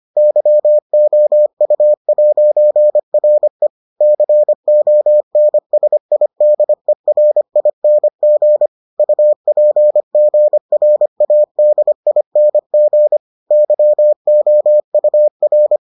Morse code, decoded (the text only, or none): YOU'RE CONSIDERING UPGRADING YOUR